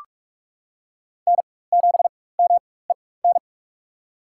Morse code, heard as a keyed tone, about 46 wpm, spaced as for 21 wpm, 700 Hz. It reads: N7MEN
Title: USA Call Signs - 5 Characters - 4x Character Spacing - 45wpm